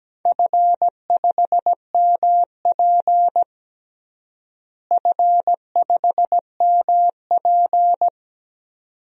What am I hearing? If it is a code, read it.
Morse code, heard as a keyed tone, about 17 wpm, 700 Hz. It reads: F5MP F5MP
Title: DXCC-100 Call Signs - 4 Characters - Repeated 2x - 17wpm